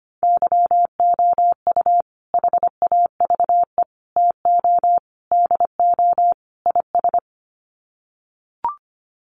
Morse code, heard as a keyed tone, about 25 wpm, 700 Hz. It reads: YOU HAVE TO DO IS
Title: Sets of 5 Words 25wpm